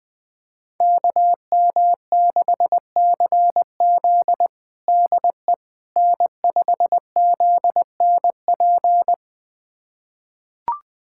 Morse code, heard as a keyed tone, about 20 wpm, 700 Hz. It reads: KM6CZ DE N5ZNP
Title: USA Callsign DE Callsign - 5 Characters - 20wpm